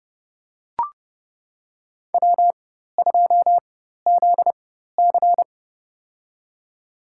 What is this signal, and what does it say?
Morse code, heard as a keyed tone, about 31 wpm, spaced as for 14 wpm, 700 Hz. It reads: W2ZC
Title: USA Call Signs - 4 Characters - 4x Character Spacing - 30wpm